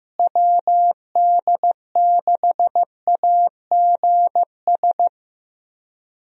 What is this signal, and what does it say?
Morse code, heard as a keyed tone, about 15 wpm, 700 Hz. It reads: WD6AGS
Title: USA Call Signs - 6 Characters - Repeated 3x - 15wpm